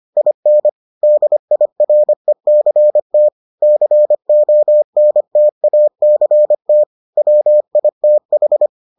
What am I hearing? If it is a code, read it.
Morse code, heard as a keyed tone, about 25 wpm, 600 Hz. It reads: IN DIRECT CONTACT WITH